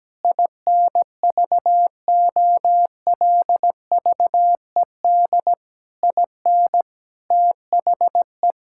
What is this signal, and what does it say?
Morse code, heard as a keyed tone, about 17 wpm, 700 Hz. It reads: INVOLVED IN THE
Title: Sentences from Top 1000 Words - Review Part 5 - 17wpm